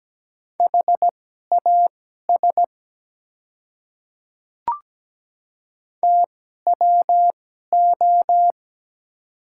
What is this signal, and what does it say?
Morse code, heard as a keyed tone, about 17 wpm, spaced as for 12 wpm, 700 Hz. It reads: HAS TWO
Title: 2x Character Spacing - Top 100 Words - 17wpm